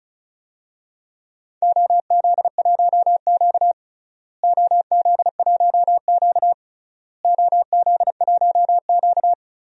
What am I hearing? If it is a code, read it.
Morse code, heard as a keyed tone, about 36 wpm, 700 Hz. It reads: OZ1Q OZ1Q OZ1Q
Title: DXCC-100 Call Signs - 4 Characters - Repeated 3x - 35wpm